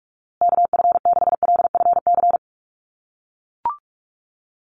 Morse code, heard as a keyed tone, about 44 wpm, 700 Hz. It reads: KF6LFC
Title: USA Call Signs Repeated 3x 45wpm